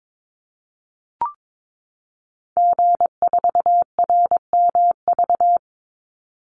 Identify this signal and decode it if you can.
Morse code, heard as a keyed tone, about 22 wpm, 700 Hz. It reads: G4RMV
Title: Call Signs Repeated 3x - 22wpm